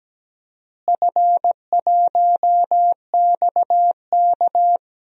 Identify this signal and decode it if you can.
Morse code, heard as a keyed tone, about 17 wpm, 700 Hz. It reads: F1XK
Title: DXCC-100 Call Signs - 4 Characters - Repeated 2x - 17wpm